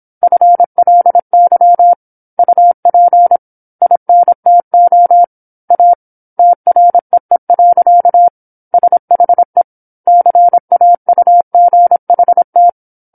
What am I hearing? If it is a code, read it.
Morse code, heard as a keyed tone, about 25 wpm, 700 Hz. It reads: FLY UP INTO A TREE. SHE CAUGHT
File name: 08-Chapter-08 26 WPM